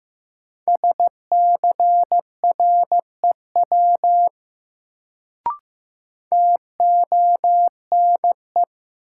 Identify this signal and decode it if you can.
Morse code, heard as a keyed tone, about 15 wpm, 700 Hz. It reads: SCREW TONE
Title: Sentences from Top 1600 Words - New Word Drill Alt - 15wpm